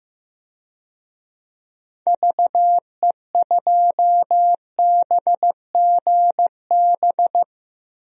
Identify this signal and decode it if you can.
Morse code, heard as a keyed tone, about 15 wpm, 700 Hz. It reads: VE2BGB
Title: DXCC-100 Call Signs - 6 Characters - 15wpm